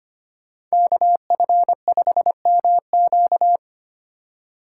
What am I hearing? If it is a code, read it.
Morse code, heard as a keyed tone, about 33 wpm, 700 Hz. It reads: KF5MQ